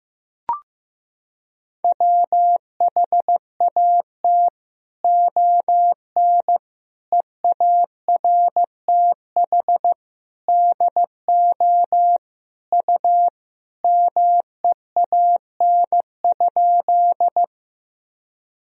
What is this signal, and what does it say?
Morse code, heard as a keyed tone, about 15 wpm, 700 Hz. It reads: WHAT ON EARTH DO U MEAN?
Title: Sentences from Top 200 Words and Abbreviations 15wpm